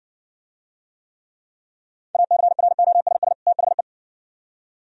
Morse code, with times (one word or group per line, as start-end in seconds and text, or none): 2.1-3.3 ACROSS
3.5-3.8 THE